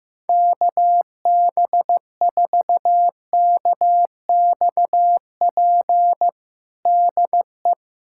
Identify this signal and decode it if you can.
Morse code, heard as a keyed tone, about 15 wpm, 700 Hz. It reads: KB4KXP DE
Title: USA Callsign DE Callsign - 6 Characters - 15wpm